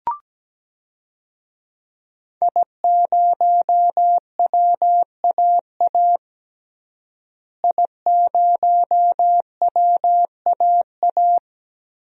Morse code, heard as a keyed tone, about 17 wpm, 700 Hz. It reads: I0WAA I0WAA
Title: DXCC-100 Call Signs - 5 Characters - Repeated 2x - 17wpm